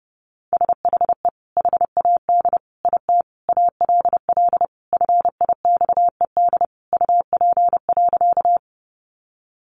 Morse code, heard as a keyed tone, about 30 wpm, 700 Hz. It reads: SHE HAD IT ALL FIXED UP.
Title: Sentences from Top 1100 Words 30wpm